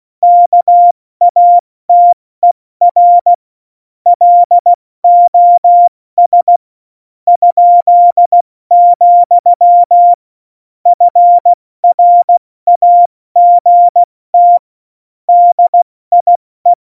Morse code, heard as a keyed tone, about 16 wpm, spaced as for 14 wpm, 700 Hz. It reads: KATER LOS?, FRAGT DIE
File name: Morsekurs / Witze / Witz 07 April 2020, 70 F80